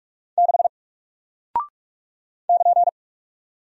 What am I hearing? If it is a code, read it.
Morse code, heard as a keyed tone, about 46 wpm, 700 Hz. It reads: 6 <KN>
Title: Single Character (Letter:Number:Punctuation:Prosign) Alt 45wpm